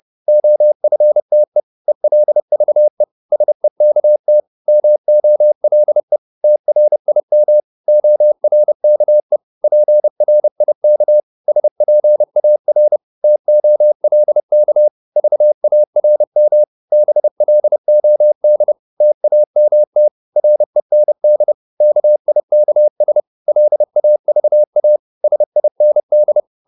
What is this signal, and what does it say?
Morse code, heard as a keyed tone, about 30 wpm, 600 Hz. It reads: OFTE ELVE SEKT MOLE TRIM ORKE PRIK SPAR TOLK VARM BLOD TAMT REND KIKS LAVA SIND
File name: Korte ord DK 30wpm